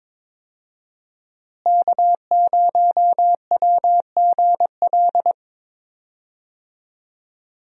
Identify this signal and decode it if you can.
Morse code, heard as a keyed tone, about 22 wpm, 700 Hz. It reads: K0WGL
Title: USA Call Signs - 5 Characters - Repeated 3x - 22wpm